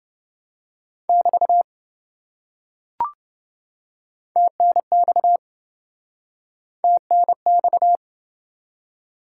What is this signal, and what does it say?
Morse code, heard as a keyed tone, about 27 wpm, 700 Hz. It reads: <BT> TNX TNX